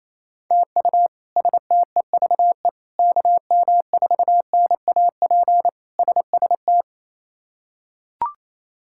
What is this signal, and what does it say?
Morse code, heard as a keyed tone, about 28 wpm, 700 Hz. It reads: TU STEVE KM4NAP SST
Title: K1USN SST QSO Practice - 28wpm